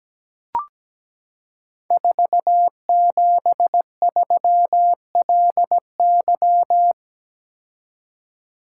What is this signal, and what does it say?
Morse code, heard as a keyed tone, about 17 wpm, 700 Hz. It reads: 473LY
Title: Code Groups - 5 Letter:Numbers - 17wpm